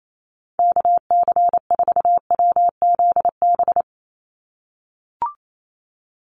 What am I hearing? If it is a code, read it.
Morse code, heard as a keyed tone, about 28 wpm, 700 Hz. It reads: KC4WZB
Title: NA SOTA Activator Callsigns 28wpm